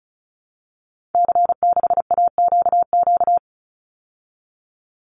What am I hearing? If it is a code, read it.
Morse code, heard as a keyed tone, about 34 wpm, 700 Hz. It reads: C6AQQ
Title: Call Signs Repeated 2x - 35wpm